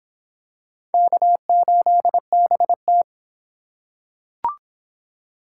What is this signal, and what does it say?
Morse code, heard as a keyed tone, about 24 wpm, 700 Hz. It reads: K8BT